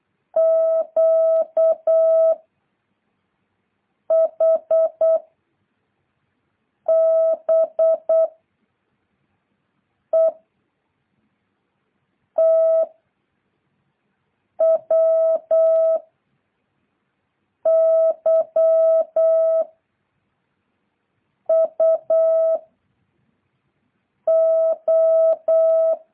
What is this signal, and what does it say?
Morse code, heard as a keyed tone, about 8 wpm, 660 Hz. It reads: Q H B E T W Y U O